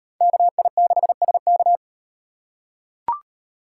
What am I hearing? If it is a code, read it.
Morse code, heard as a keyed tone, about 38 wpm, 700 Hz. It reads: KI6SK